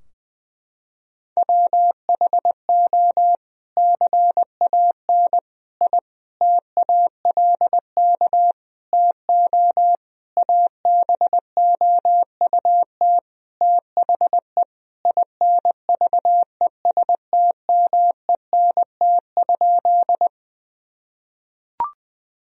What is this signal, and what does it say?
Morse code, heard as a keyed tone, about 21 wpm, 700 Hz. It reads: WHO CAN I TALK TO ABOUT THE INVESTMENT?
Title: Sentences from Top 1000 Words - Review Part 4 - 20wpm